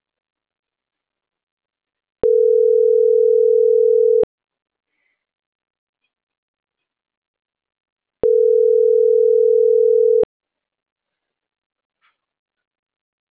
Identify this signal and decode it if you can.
Ringback tone (North America), 440 + 480 Hz.